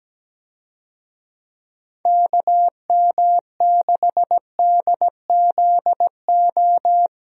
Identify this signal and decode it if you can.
Morse code, heard as a keyed tone, about 17 wpm, 700 Hz. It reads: KM6DZO
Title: USA Call Signs - 6 Characters - 17wpm